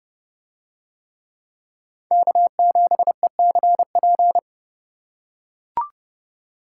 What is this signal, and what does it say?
Morse code, heard as a keyed tone, about 30 wpm, 700 Hz. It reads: K7ECP